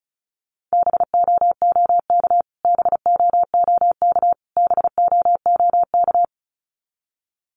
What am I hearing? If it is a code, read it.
Morse code, heard as a keyed tone, about 34 wpm, 700 Hz. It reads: BOOK BOOK BOOK
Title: Top 500 Words Repeated 35wpm